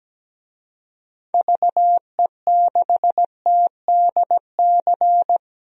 Morse code, heard as a keyed tone, about 17 wpm, 700 Hz. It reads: VE6TDC
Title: DXCC-100 Call Signs - 6 Characters - 17wpm